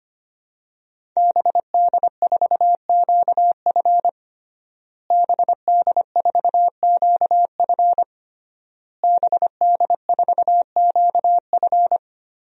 Morse code, heard as a keyed tone, about 25 wpm, 700 Hz. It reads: BD4QF BD4QF BD4QF
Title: DXCC-100 Call Signs - 5 Characters - Repeated 3x - 25wpm